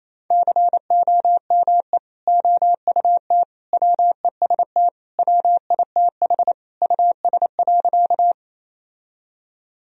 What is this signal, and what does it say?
Morse code, heard as a keyed tone, about 27 wpm, 700 Hz. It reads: COME OUT WEST WITH US.
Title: Sentences from Top 1000 Words - Review Part 2 - 28wpm